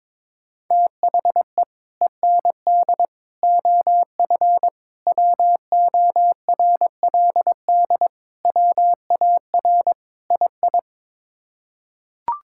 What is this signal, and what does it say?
Morse code, heard as a keyed tone, about 21 wpm, 700 Hz. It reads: THE END OF WORLD WAR II
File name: practice / 22wpm / Sets of 6 Words - 22wpm